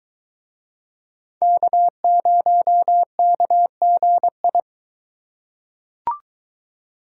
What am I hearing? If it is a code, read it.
Morse code, heard as a keyed tone, about 22 wpm, 700 Hz. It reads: K0KGI